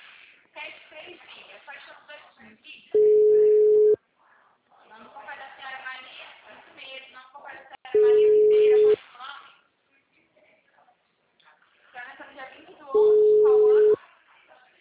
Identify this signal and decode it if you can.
Ringback tone (Europe, 425 Hz), with speech.
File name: out- -2010- - - .43641